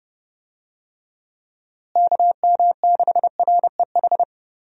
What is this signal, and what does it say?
Morse code, heard as a keyed tone, about 30 wpm, 700 Hz. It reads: KM6REH
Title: USA Call Signs - 6 Characters - Repeated 2x - 30wpm